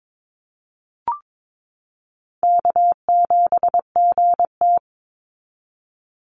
Morse code, heard as a keyed tone, about 22 wpm, 700 Hz. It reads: K7GT
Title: NA SOTA Chaser Callsigns 22wpm